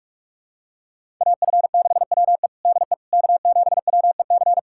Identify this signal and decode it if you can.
Morse code, heard as a keyed tone, about 46 wpm, 700 Hz. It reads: AF6WE DE K7WEC
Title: USA Callsign DE Callsign - 5 Characters - 45wpm